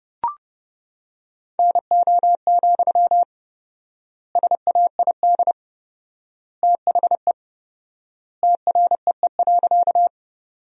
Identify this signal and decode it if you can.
Morse code, heard as a keyed tone, about 30 wpm, 700 Hz. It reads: NO, SAID THE TREE.
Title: 4x Word-Spacing - Sentences from Top 300 Words - 30wpm